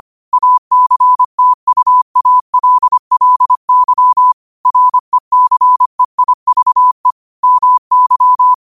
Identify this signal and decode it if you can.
Morse code, heard as a keyed tone, about 25 wpm, 1000 Hz. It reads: ACTUALLY RECEIVE MY
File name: qotdfiles / 25wpm